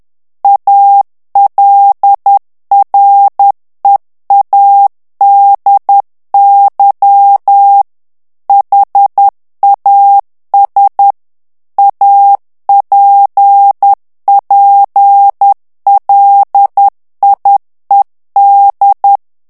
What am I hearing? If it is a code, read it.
Morse code, heard as a keyed tone, about 10 wpm, 800 Hz. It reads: ALREADY HAS APPLIED
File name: ANGLAIS texte clair 3 vitesse 10 mots minute